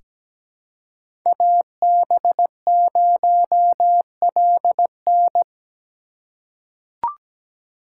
Morse code, heard as a keyed tone, about 17 wpm, 700 Hz. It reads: AB0LN
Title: USA Call Signs - 5 Characters - 4x Character Spacing - 17wpm